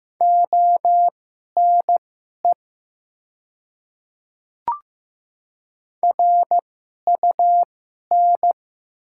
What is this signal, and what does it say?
Morse code, heard as a keyed tone, about 15 wpm, spaced as for 11 wpm, 700 Hz. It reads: ONE RUN
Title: 2x Character Spacing - Top 500 Words - 15wpm